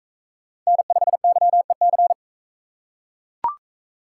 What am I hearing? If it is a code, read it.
Morse code, heard as a keyed tone, about 41 wpm, 700 Hz. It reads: N5YEC